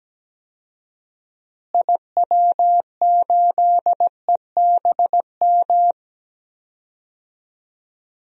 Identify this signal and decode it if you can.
Morse code, heard as a keyed tone, about 17 wpm, 700 Hz. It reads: IW8EBM
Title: DXCC-100 Call Signs - 6 Characters - Repeated 3x - 17wpm